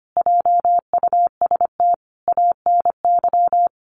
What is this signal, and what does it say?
Morse code, heard as a keyed tone, about 25 wpm, 700 Hz. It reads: JUST ANY
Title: A Sound Of Thunder 25wpm